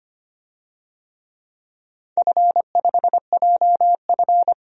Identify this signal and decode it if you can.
Morse code, heard as a keyed tone, about 25 wpm, 700 Hz. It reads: F5JF
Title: DXCC-100 Call Signs - 4 Characters - 25wpm